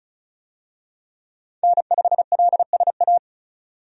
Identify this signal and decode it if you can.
Morse code, heard as a keyed tone, about 36 wpm, 700 Hz. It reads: N5LSA
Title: USA Call Signs - 5 Characters - 35wpm